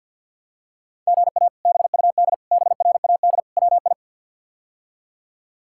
Morse code, heard as a keyed tone, about 46 wpm, 700 Hz. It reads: GA BUD BRAD WI
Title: K1USN SST QSO Practice - 50wpm